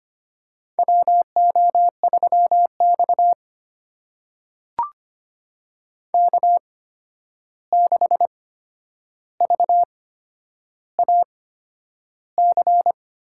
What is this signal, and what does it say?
Morse code, heard as a keyed tone, about 25 wpm, spaced as for 10 wpm, 700 Hz. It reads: WO3X K6VAC
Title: USA Call Signs - 8x Character Spacing - 25wpm